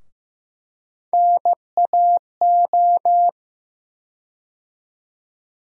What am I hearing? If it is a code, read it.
Morse code, heard as a keyed tone, about 15 wpm, 700 Hz. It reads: NAO
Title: USA Call Sign Postfixes - 3 Letters 15wpm